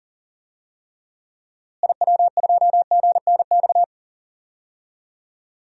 Morse code, heard as a keyed tone, about 40 wpm, 700 Hz. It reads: IW2GNX
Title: DXCC-100 Call Signs - 6 Characters - Repeated 3x - 40wpm